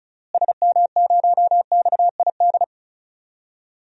Morse code, heard as a keyed tone, about 36 wpm, 700 Hz. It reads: SM0XID
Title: DXCC-100 Call Signs - 6 Characters - Repeated 2x - 35wpm